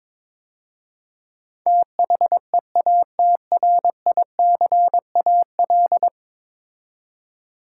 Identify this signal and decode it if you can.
Morse code, heard as a keyed tone, about 21 wpm, 700 Hz. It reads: THEATRICAL
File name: practice / 22wpm / 10 Letter Words - First 3 Letters Given - 22wpm